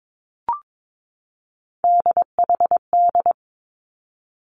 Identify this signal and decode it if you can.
Morse code, heard as a keyed tone, about 22 wpm, 700 Hz. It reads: DHD
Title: Three Challenging Letter-Number Combinations 22wpm